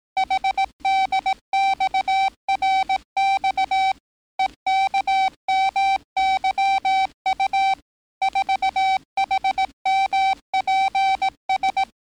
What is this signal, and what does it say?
Morse code, heard as a keyed tone, about 18 wpm, 770 Hz. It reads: HDXRX EKMYU 4HMPS
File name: training / NAVPERS 11049to11060 / NAVPERS 11058 RA RB